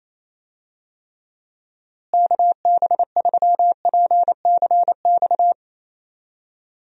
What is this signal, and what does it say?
Morse code, heard as a keyed tone, about 27 wpm, 700 Hz. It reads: KB3PCX
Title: USA Call Signs - 6 Characters - Repeated 3x - 28wpm